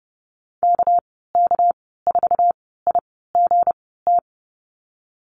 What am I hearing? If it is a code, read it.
Morse code, heard as a keyed tone, about 30 wpm, spaced as for 17 wpm, 700 Hz. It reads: KK4IGT
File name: Lesson 37g - Letter x - Call Signs 30wpm